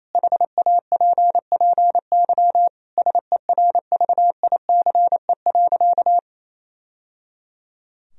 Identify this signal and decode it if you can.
Morse code, heard as a keyed tone, about 28 wpm, 700 Hz. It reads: HAPPY SERVICE.
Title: Sentences from Top 1000 Words - Review Part 4 - 28wpm